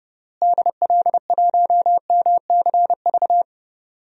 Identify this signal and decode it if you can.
Morse code, heard as a keyed tone, about 30 wpm, 700 Hz. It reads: DL1MCV
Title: DXCC-100 Call Signs - 6 Characters - Repeated 3x - 30wpm